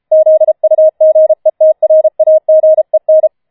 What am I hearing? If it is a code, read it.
Morse code, heard as a keyed tone, about 33 wpm, 630 Hz. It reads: ZUGETRAGEN
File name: Schneewittchen Teil5 34wpm